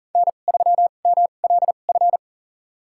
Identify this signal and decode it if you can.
Morse code, heard as a keyed tone, about 40 wpm, spaced as for 29 wpm, 700 Hz. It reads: N3MLF
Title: USA Call Signs - 5 Characters - 2x Character Spacing - 40wpm